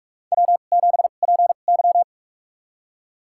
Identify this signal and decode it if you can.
Morse code, heard as a keyed tone, about 46 wpm, spaced as for 33 wpm, 700 Hz. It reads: W7PY
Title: USA Call Signs - 4 Characters - 2x Character Spacing - 45wpm